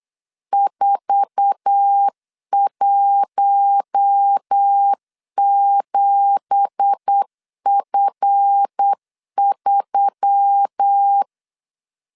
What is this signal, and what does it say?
Morse code, heard as a keyed tone, about 8 wpm, 800 Hz. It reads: 417F3
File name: LEZIONI / 10 SIGNS / 10A SLASH 8wpm 3452